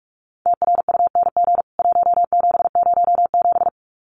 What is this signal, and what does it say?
Morse code, heard as a keyed tone, about 44 wpm, 700 Hz. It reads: TRUNG 1707
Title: CWT Exchange 45wpm